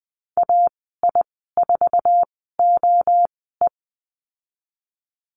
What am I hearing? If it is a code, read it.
Morse code, heard as a keyed tone, about 20 wpm, spaced as for 15 wpm, 700 Hz. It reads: AI4OE
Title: 048 Lesson 9g - Number 4 - Call Signs 20wpm